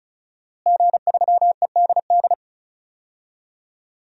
Morse code, heard as a keyed tone, about 36 wpm, 700 Hz. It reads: G3EDD
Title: DXCC-100 Call Signs - 5 Characters - Repeated 3x - 35wpm